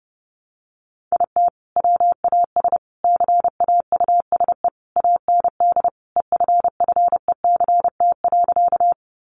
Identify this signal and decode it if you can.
Morse code, heard as a keyed tone, about 30 wpm, 700 Hz. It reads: IT WAS CAUSE AND EFFECT.
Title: Sentences from Top 1000 Words 30wpm